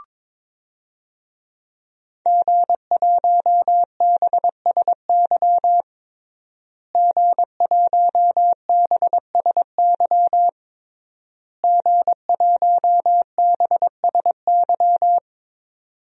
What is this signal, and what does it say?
Morse code, heard as a keyed tone, about 22 wpm, 700 Hz. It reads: G1BSY G1BSY G1BSY
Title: DXCC-100 Call Signs - 5 Characters - Repeated 3x - 22wpm